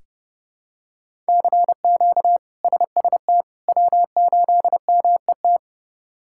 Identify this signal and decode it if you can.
Morse code, heard as a keyed tone, about 32 wpm, 700 Hz. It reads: CQ SST W8MET
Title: K1USN SST QSO Practice - 30wpm